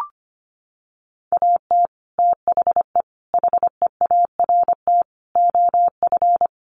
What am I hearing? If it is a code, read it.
Morse code, heard as a keyed tone, about 25 wpm, 700 Hz. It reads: AT THE HEART OF
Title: Sets of 4 Words 25wpm